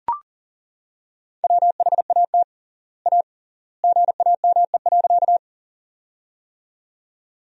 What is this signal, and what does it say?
Morse code, heard as a keyed tone, about 37 wpm, 700 Hz. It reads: WHAT A GAME.
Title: 3x Word-Spacing - Sentences from Top 500 Words - 40wpm